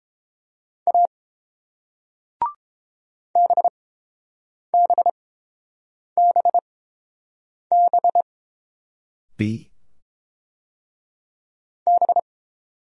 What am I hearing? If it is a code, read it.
Morse code, heard as a keyed tone, about 30 wpm, spaced as for 7 wpm, 700 Hz. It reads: A BBBB B